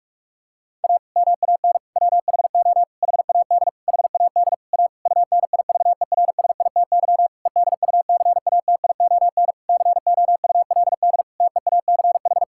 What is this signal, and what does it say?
Morse code, heard as a keyed tone, about 46 wpm, 700 Hz. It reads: A MAN WHO HAD HAD A UNIVERSITY EDUCATION COULD TEACH
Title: Sentences from Top 1000 Words - Review Part 6 - 45wpm